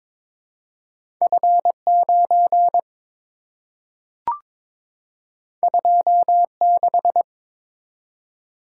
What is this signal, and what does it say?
Morse code, heard as a keyed tone, about 22 wpm, 700 Hz. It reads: F9 26
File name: practice / 22wpm / Two Letter-Number Combinations - 22wpm